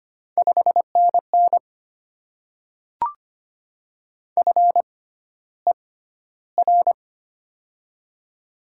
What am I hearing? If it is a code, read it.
Morse code, heard as a keyed tone, about 25 wpm, spaced as for 13 wpm, 700 Hz. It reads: 5NN FER
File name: Top 20 QSO Elements - 6x Character Spacing - 25wpm